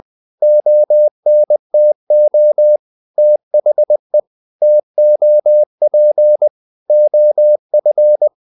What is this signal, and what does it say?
Morse code, heard as a keyed tone, about 20 wpm, 600 Hz. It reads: ONTO THE TOP OF
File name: the old man and the sea part 2 20wpm